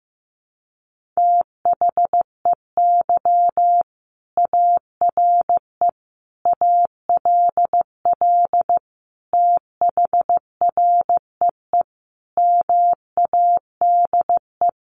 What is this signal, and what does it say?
Morse code, heard as a keyed tone, about 15 wpm, 700 Hz. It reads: THEY ARE ALL THREE MADE
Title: Sentences from Top 1500 Words 15wpm